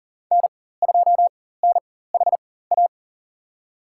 Morse code, heard as a keyed tone, about 40 wpm, spaced as for 19 wpm, 700 Hz. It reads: N2NHA
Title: USA Call Signs - 5 Characters - 4x Character Spacing - 40wpm